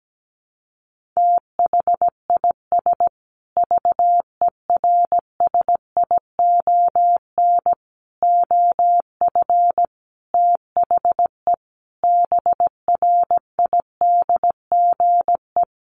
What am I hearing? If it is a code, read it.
Morse code, heard as a keyed tone, about 17 wpm, 700 Hz. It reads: THIS VERSION OF THE BRIDGE
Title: Sentences from Top 1400 Words 17wpm